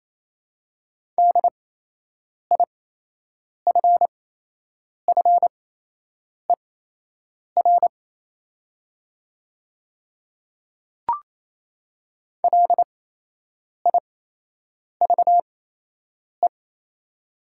Morse code, heard as a keyed tone, about 28 wpm, spaced as for 8 wpm, 700 Hz. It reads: DIFFER LIVE
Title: 8x Character Spacing - Top 200 Words - 28wpm